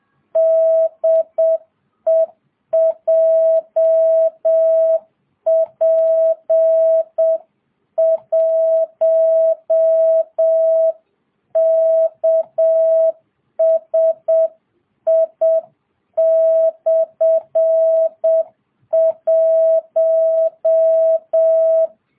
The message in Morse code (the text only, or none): DE JP1KSI/1